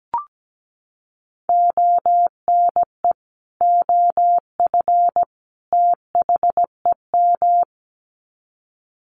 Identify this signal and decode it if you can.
Morse code, heard as a keyed tone, about 17 wpm, 700 Hz. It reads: ONE OF THEM
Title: Sets of 3 Words 17wpm